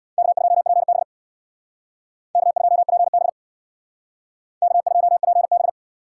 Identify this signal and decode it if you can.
Morse code, heard as a keyed tone, about 62 wpm, 700 Hz. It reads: K2PB K2PB K2PB
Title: Speed Racing - USA Call Signs 50wpm